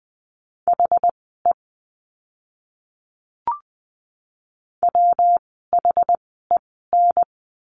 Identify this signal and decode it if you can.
Morse code, heard as a keyed tone, about 20 wpm, spaced as for 14 wpm, 700 Hz. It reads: HE WHEN
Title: 115 Lesson 19f - Letter w - Words 20wpm